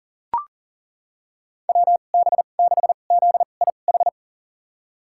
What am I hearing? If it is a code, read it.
Morse code, heard as a keyed tone, about 37 wpm, spaced as for 28 wpm, 700 Hz. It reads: WB6ZIH
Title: USA Call Signs - 6 Characters - 2x Character Spacing - 40wpm